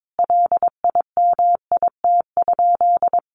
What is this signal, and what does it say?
Morse code, heard as a keyed tone, about 22 wpm, 700 Hz. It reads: LIMIT?
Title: Sentences from Top 1500 Words 22wpm